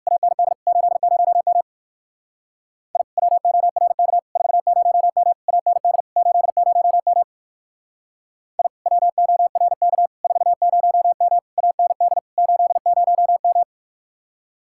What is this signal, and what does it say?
Morse code, heard as a keyed tone, about 55 wpm, 700 Hz. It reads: AND 80M I WORK 40M AND 80M I WORK 40M AND 80M
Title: Speed-Racing Common QSO Phrases Encore - 45wpm